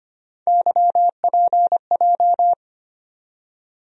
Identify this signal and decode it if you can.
Morse code, heard as a keyed tone, about 25 wpm, 700 Hz. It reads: YPJ
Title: USA Call Sign Postfixes - 3 Letters 25wpm